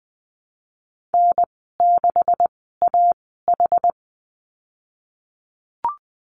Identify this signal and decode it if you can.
Morse code, heard as a keyed tone, about 20 wpm, spaced as for 14 wpm, 700 Hz. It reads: N6AH
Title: 128 Lesson 21g - Number 6 - Call Signs 20wpm